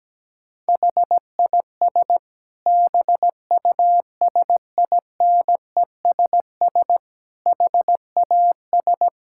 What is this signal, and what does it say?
Morse code, heard as a keyed tone, about 17 wpm, 700 Hz. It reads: HIS BUSINESS HAS
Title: Sentences from Top 1000 Words - Review Part 6 - 17wpm